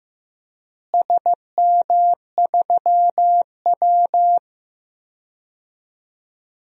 Morse code, heard as a keyed tone, about 15 wpm, 700 Hz. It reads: SM3W
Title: DXCC-100 Call Signs - 4 Characters - Repeated 3x - 15wpm